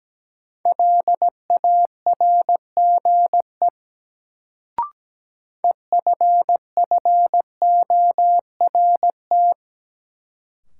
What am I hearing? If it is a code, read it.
Morse code, heard as a keyed tone, about 17 wpm, 700 Hz. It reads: LARGE EFFORT
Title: Sets of 4 Words - New Word Drill - 17wpm